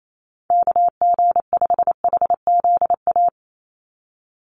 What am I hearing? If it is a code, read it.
Morse code, heard as a keyed tone, about 28 wpm, 700 Hz. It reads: KG5HZA
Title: USA Call Signs 28wpm